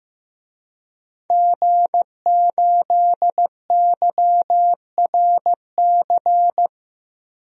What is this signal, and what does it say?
Morse code, heard as a keyed tone, about 16 wpm, 700 Hz. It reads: G8YRC
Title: DXCC-100 Call Signs - 5 Characters - Repeated 2x - 15wpm